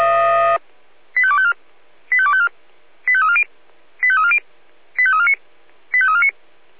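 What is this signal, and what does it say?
Five-tone selective call (ZVEI), heard as quick runs of five tones, tones of about 70 ms. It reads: 85254 85254 85340 85340 85349 85349
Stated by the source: ZVEI five-tone call received by SDR.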